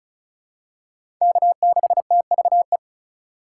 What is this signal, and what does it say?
Morse code, heard as a keyed tone, about 36 wpm, 700 Hz. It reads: K6TVE
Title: USA Call Signs - 4x Character Spacing - 35wpm